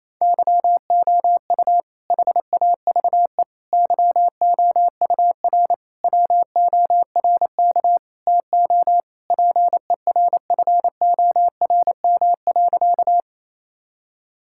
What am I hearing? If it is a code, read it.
Morse code, heard as a keyed tone, about 28 wpm, 700 Hz. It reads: YOU HAVE YOUR WORK TO PERFORM.
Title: Sentences from Top 1000 Words - Review Part 5 - 28wpm